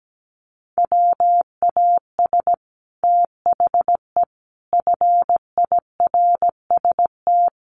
Morse code, heard as a keyed tone, about 17 wpm, 700 Hz. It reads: WAS THE FIRST
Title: Sets of 4 Words 17wpm